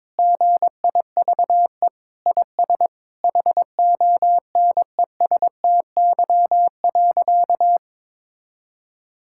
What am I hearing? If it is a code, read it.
Morse code, heard as a keyed tone, about 22 wpm, 700 Hz. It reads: GIVE IS HONESTY.
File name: practice / 22wpm / Sentences with 8-Word Phrases - 22wpm